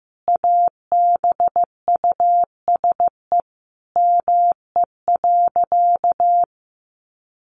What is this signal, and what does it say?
Morse code, heard as a keyed tone, about 15 wpm, 700 Hz. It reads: ABUSE ME.
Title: Sentences from Top 1300 Words 15wpm